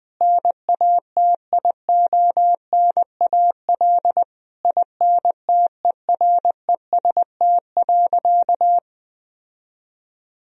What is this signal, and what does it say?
Morse code, heard as a keyed tone, about 21 wpm, 700 Hz. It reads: NATIONAL INTEREST.
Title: Sentences from Top 1000 Words - Review Part 4 - 20wpm